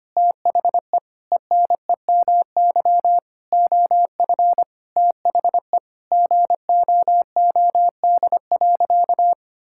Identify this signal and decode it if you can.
Morse code, heard as a keyed tone, about 25 wpm, 700 Hz. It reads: THE ENEMY OF THE GOOD.
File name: practice / 25wpm / Sentences with 9-Word Phrases - 25wpm